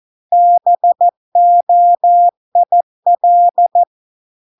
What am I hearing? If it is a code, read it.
Morse code, heard as a keyed tone, about 14 wpm, 700 Hz. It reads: BOIL